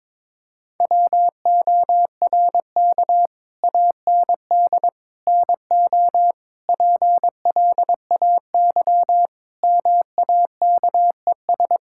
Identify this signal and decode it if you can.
Morse code, heard as a keyed tone, about 21 wpm, 700 Hz. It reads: WORK AND NO PLAY MAKES
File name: practice / 22wpm / Sets of 8 Words - 22wpm